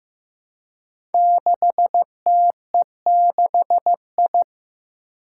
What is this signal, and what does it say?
Morse code, heard as a keyed tone, about 15 wpm, 700 Hz. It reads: 6TE6I
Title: Five Challenging Letter-Number Combinations 15wpm